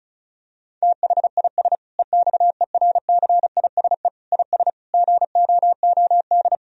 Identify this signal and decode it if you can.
Morse code, heard as a keyed tone, about 36 wpm, 700 Hz. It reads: THIS EXERCISE IS GOOD
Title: Sentences from Top 1000 Words - Review Part 4 - 35wpm